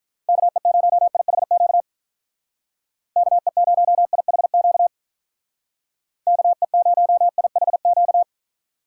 Morse code, heard as a keyed tone, about 46 wpm, 700 Hz. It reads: KE0IHQ KE0IHQ KE0IHQ